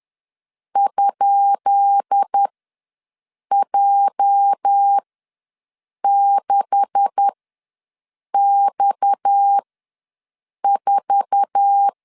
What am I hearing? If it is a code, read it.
Morse code, heard as a keyed tone, about 10 wpm, spaced as for 6 wpm, 800 Hz. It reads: ?J6X4
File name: LEZIONI / 10 SIGNS / 10B INT ws 10wpm 4134